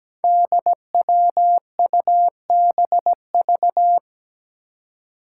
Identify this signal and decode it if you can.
Morse code, heard as a keyed tone, about 17 wpm, 700 Hz. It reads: DWUBV
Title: Five Challenging Letter-Number Combinations 17wpm